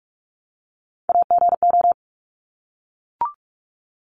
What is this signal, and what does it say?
Morse code, heard as a keyed tone, about 44 wpm, 700 Hz. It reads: AGO
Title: Top 500 Words Repeated 45wpm